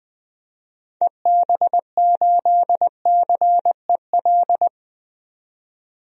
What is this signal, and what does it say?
Morse code, heard as a keyed tone, about 20 wpm, 700 Hz. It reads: EB8CEL
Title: DXCC-100 Call Signs - 6 Characters - 20wpm